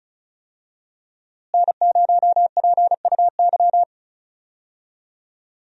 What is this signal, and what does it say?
Morse code, heard as a keyed tone, about 34 wpm, 700 Hz. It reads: N0PUY
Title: USA Call Signs - 5 Characters - Repeated 2x - 35wpm